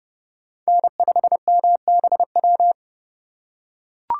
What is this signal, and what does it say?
Morse code, heard as a keyed tone, about 32 wpm, 700 Hz. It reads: N5MBW
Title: USA Call Signs - 5 Characters - 3x Character Spacing - 30wpm